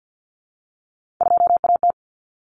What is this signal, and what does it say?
Morse code, heard as a keyed tone, about 50 wpm, 700 Hz. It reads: 2AT
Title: Common Cut Numbers - 3-Digit 50wpm